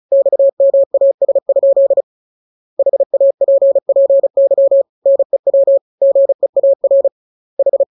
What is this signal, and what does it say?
Morse code, heard as a keyed tone, about 35 wpm, 550 Hz. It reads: XMAS? HAPPY NEW GEAR H
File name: jokes / joke 17 35 0000